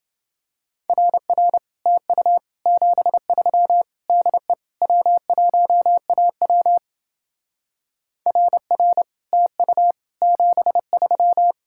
Morse code, heard as a keyed tone, about 28 wpm, 700 Hz. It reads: RR TU 73 DE W1AW RR TU 73